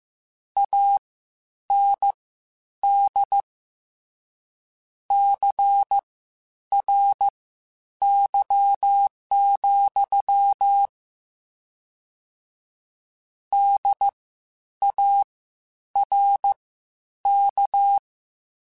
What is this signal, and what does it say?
Morse code, heard as a keyed tone, about 15 wpm, spaced as for 9 wpm, 800 Hz. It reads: AND CRY, DARK
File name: Misc / Misc-mp3s / WordPairs sp @13